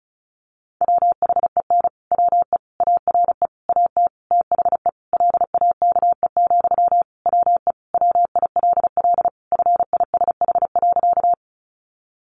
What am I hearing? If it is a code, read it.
Morse code, heard as a keyed tone, about 34 wpm, 700 Hz. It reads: WHEN WE ARE AT THE LAKE, WE WILL FISH.
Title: Sentences from Top 1100 Words 35wpm